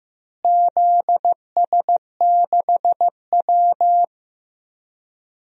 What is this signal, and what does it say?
Morse code, heard as a keyed tone, about 15 wpm, 700 Hz. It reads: ZS6W
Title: DXCC-100 Call Signs - 4 Characters - Repeated 3x - 15wpm